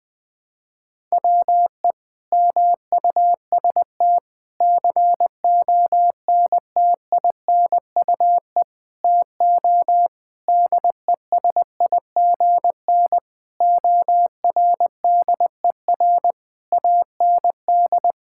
Morse code, heard as a keyed tone, about 20 wpm, 700 Hz. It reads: WE MUST CONTINUE TO DESIGN ORDER AND
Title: Sentences from Top 1000 Words - Review Part 6 - 20wpm